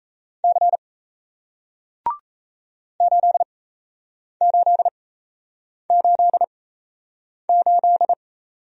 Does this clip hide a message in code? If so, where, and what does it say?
Yes: Morse code from 0.4 to 8.1 s, heard as a keyed tone, about 38 wpm, spaced as for 8 wpm, 700 Hz. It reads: C 8888